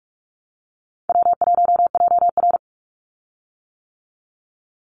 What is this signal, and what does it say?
Morse code, heard as a keyed tone, about 46 wpm, 700 Hz. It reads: W1JR
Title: Call Signs - 45wpm